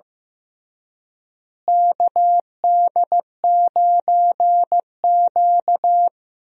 Morse code, heard as a keyed tone, about 15 wpm, 700 Hz. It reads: KD9Q
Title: USA Call Signs - 4 Characters - Repeated 3x - 15wpm